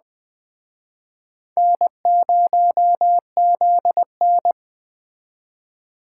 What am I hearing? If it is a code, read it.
Morse code, heard as a keyed tone, about 21 wpm, 700 Hz. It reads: N0ZN